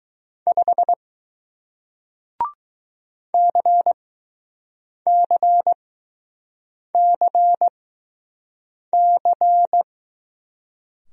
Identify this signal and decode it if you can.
Morse code, heard as a keyed tone, about 20 wpm, spaced as for 6 wpm, 700 Hz. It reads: 5 CCCC